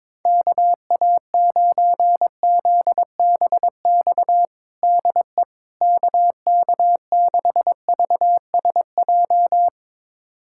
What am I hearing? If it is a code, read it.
Morse code, heard as a keyed tone, about 22 wpm, 700 Hz. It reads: KA9ZBX DE KK6VSJ
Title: USA Callsign DE Callsign - 6 Characters - 22wpm